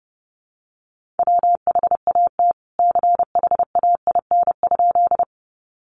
Morse code, heard as a keyed tone, about 30 wpm, 700 Hz. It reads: WHAT CHAIN?
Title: Sentences from Top 1500 Words 30wpm